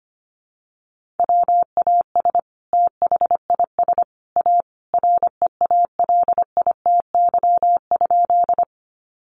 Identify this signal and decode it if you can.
Morse code, heard as a keyed tone, about 25 wpm, 700 Hz. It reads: WAS THIS A REALITY?
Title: Sentences from Top 1200 Words 25wpm